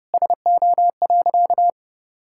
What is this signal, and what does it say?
Morse code, heard as a keyed tone, about 32 wpm, 700 Hz. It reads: SO.
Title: Sentences from Top 1000 Words - Review Part 6 - 30wpm